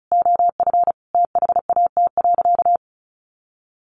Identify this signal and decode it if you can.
Morse code, heard as a keyed tone, about 34 wpm, 700 Hz. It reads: OF THAT.
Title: Sentences from Top 1500 Words 35wpm